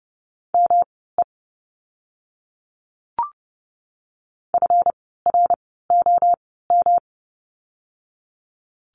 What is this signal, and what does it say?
Morse code, heard as a keyed tone, about 30 wpm, spaced as for 17 wpm, 700 Hz. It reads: ME FROM
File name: course / 30:10wpm / 237 Lesson 38f - Letter q - Words 30wpm